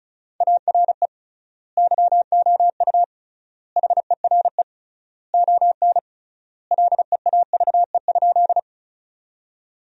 Morse code, heard as a keyed tone, about 36 wpm, 700 Hz. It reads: ARE YOU HERE ON LEAVE?
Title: 3x Word-Spacing - Sentences from Top 400 Words - 35wpm